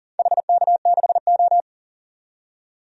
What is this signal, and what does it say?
Morse code, heard as a keyed tone, about 40 wpm, 700 Hz. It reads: HK6O
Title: DXCC-100 Call Signs - 4 Characters - 40wpm